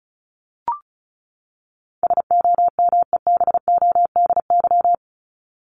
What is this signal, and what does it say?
Morse code, heard as a keyed tone, about 34 wpm, 700 Hz. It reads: SOMEBODY
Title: Top 1751-2000 Words 35wpm